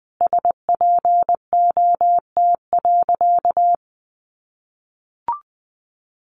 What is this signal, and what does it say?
Morse code, heard as a keyed tone, about 20 wpm, 700 Hz. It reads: SPOT.
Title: Sentences from Top 1200 Words 20wpm